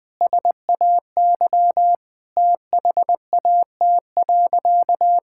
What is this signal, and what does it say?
Morse code, heard as a keyed tone, about 21 wpm, 700 Hz. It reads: SAY THAT.
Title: Sentences with 5-Word Phrases - 20wpm